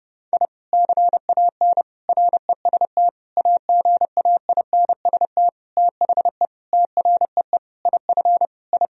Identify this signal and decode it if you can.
Morse code, heard as a keyed tone, about 30 wpm, 700 Hz. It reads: I CAN REST AGAINST THE TREE IF I
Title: Sentences from Top 1000 Words - Review Part 1 - 30wpm